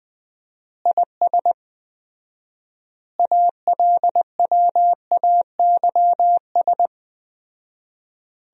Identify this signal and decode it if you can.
Morse code, heard as a keyed tone, about 19 wpm, 700 Hz. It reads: IS ALWAYS
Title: 4x Word-Spacing - Sentences from Top 400 Words - 20wpm